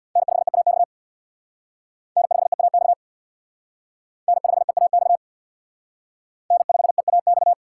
Morse code, heard as a keyed tone, about 61 wpm, 700 Hz. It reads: N5EAX N5EAX N5EAX N5EAX
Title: Speed Racing - USA 1x3 Call Signs 50wpm